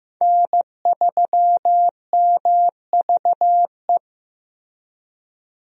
Morse code, heard as a keyed tone, about 15 wpm, 700 Hz. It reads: N3MVE
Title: USA Call Signs - 5 Characters - Repeated 3x - 15wpm